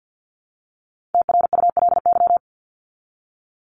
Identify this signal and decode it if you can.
Morse code, heard as a keyed tone, about 51 wpm, 700 Hz. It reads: TRULY
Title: LY Suffix - Top 100 Words with Base Words 50wpm